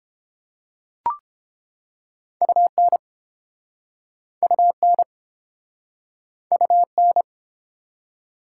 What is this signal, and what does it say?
Morse code, heard as a keyed tone, about 34 wpm, 700 Hz. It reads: UN UN UN